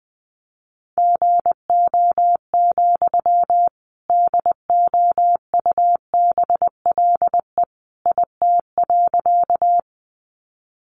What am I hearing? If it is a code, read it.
Morse code, heard as a keyed tone, about 20 wpm, 700 Hz. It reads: GO, DOUBLE IT.
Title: Sentences from Top 1400 Words 20wpm